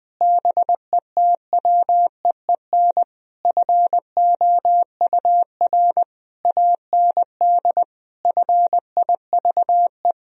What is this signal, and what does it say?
Morse code, heard as a keyed tone, about 21 wpm, 700 Hz. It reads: BETWEEN FOUR AND FIVE
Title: Sentences from Top 1000 Words - Review Part 2 - 20wpm